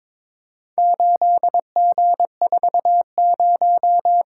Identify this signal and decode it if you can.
Morse code, heard as a keyed tone, about 22 wpm, 700 Hz. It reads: 8G40
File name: practice / 22wpm / Four Letter-Number Combinations - 22wpm